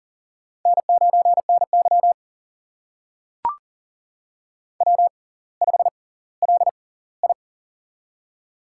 Morse code, heard as a keyed tone, about 38 wpm, spaced as for 20 wpm, 700 Hz. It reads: N9NY W5LI
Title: USA Call Signs - 6x Character Spacing - 40wpm